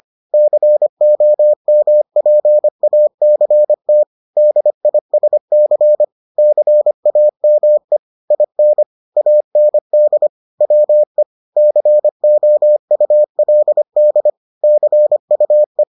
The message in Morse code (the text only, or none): COMPACT DISC CAME IN AND WE COULD CUE